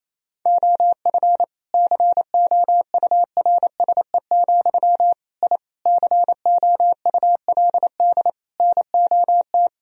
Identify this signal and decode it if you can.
Morse code, heard as a keyed tone, about 28 wpm, 700 Hz. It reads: OF COURSE, I COULD NOT
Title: Sentences from Top 1000 Words - Review Part 3 - 28wpm